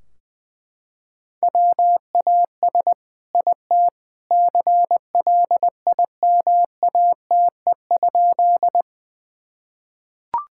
Morse code, heard as a keyed tone, about 21 wpm, 700 Hz. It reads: WAS IT CLIMATE?
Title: Sentences from Top 1600 Words 20wpm